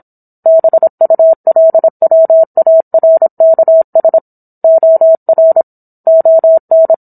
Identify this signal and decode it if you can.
Morse code, heard as a keyed tone, about 27 wpm, 650 Hz. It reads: BULWARKS OR ON